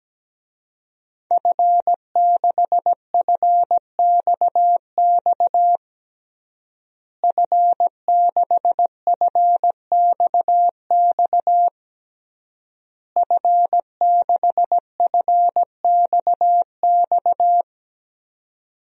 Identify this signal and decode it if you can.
Morse code, heard as a keyed tone, about 17 wpm, 700 Hz. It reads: F6FXX F6FXX F6FXX
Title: DXCC-100 Call Signs - 5 Characters - Repeated 3x - 17wpm